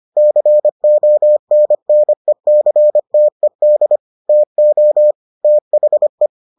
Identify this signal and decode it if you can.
Morse code, heard as a keyed tone, about 25 wpm, 600 Hz. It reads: CONNECTED TO THE